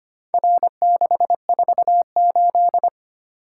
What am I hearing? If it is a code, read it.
Morse code, heard as a keyed tone, about 25 wpm, 700 Hz. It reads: R648
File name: practice / 25wpm / Four Letter-Number Combinations - 25wpm